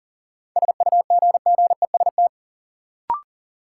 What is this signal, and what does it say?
Morse code, heard as a keyed tone, about 40 wpm, 700 Hz. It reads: SUGGEST
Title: Sentences from Top 1300 Words - New Word Drill Alt - 40wpm